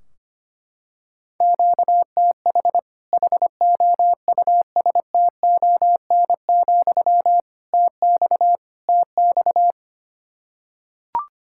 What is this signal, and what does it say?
Morse code, heard as a keyed tone, about 25 wpm, 700 Hz. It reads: QTH HOUSTON, TX TX
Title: Practice QSO - Exchange 1 - Beginning - 25wpm